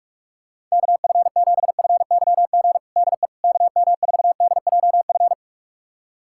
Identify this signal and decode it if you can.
Morse code, heard as a keyed tone, about 46 wpm, 700 Hz. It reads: KF7FYG DE KM4DJF
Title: USA Callsign DE Callsign 45wpm